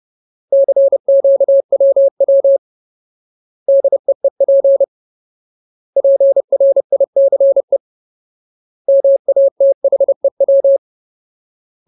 Morse code, heard as a keyed tone, about 30 wpm, 550 Hz. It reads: CQWW DEEP PRICE MATHEW
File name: Ultimate Word List / 30WPM Ultimate